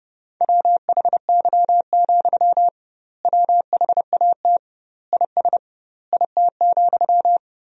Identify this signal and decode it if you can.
Morse code, heard as a keyed tone, about 32 wpm, 700 Hz. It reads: WHY, WHAT IS IT,
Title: 2x Word-Spacing - Sentences from Top 200 Words - 30wpm